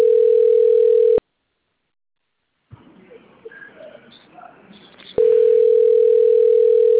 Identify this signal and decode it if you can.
Ringback tone (North America), 440 + 480 Hz.